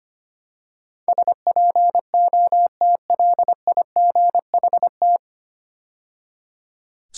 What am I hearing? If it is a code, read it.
Morse code, heard as a keyed tone, about 25 wpm, 700 Hz. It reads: SPOTLIGHT
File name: practice / 25wpm / 9 Letter Words - First 3 Letters Given - 25wpm